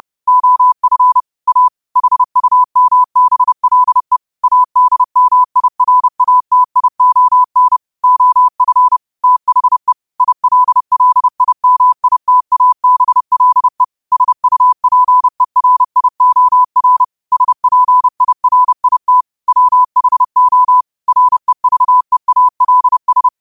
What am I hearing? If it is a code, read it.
Morse code, heard as a keyed tone, about 30 wpm, 1000 Hz. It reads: OF A HUMBLE ADMIRATION OF THE ILLIMITABLE SUPERIOR SPIRIT WHO REVEALS